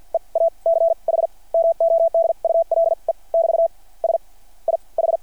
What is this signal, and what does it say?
Morse code, heard as a keyed tone, about 53 wpm, spaced as for 33 wpm, 660 Hz. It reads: EACH MODULE <BT> SIH
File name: archived / CRSnet